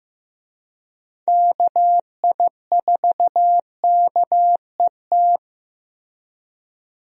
Morse code, heard as a keyed tone, about 15 wpm, 700 Hz. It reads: KI4KET
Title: USA Call Signs - 6 Characters - Repeated 3x - 15wpm